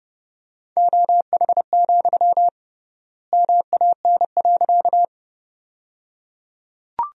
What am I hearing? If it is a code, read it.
Morse code, heard as a keyed tone, about 30 wpm, 700 Hz. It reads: OH, MAN.
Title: 3x Word-Spacing - Sentences from Top 500 Words - 30wpm